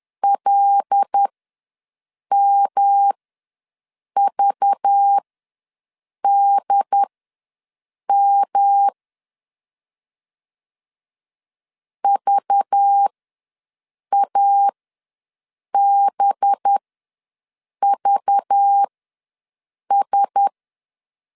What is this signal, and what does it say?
Morse code, heard as a keyed tone, about 10 wpm, spaced as for 6 wpm, 800 Hz. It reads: LMVDM VABVS
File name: LEZIONI / 5 LBVC / 5 LBV ws 10wpm 3645